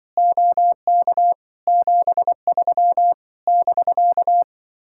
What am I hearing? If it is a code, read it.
Morse code, heard as a keyed tone, about 24 wpm, 700 Hz. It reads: OK 73 <BK>
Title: Speed-Racing Common QSO Phrases - 20wpm